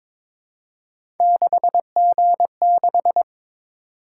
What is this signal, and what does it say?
Morse code, heard as a keyed tone, about 22 wpm, 700 Hz. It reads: 6G6
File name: practice / 22wpm / Three Letter-Number Combinations - 22wpm